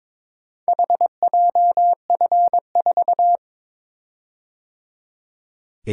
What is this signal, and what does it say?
Morse code, heard as a keyed tone, about 22 wpm, 700 Hz. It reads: HJF4
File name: practice / 22wpm / Four Letter-Number Combinations - 22wpm